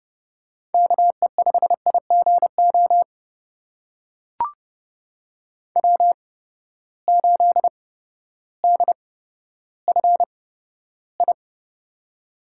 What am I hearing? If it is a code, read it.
Morse code, heard as a keyed tone, about 30 wpm, spaced as for 14 wpm, 700 Hz. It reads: KE5IGO W8DFI
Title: USA Call Signs - 8x Character Spacing - 30wpm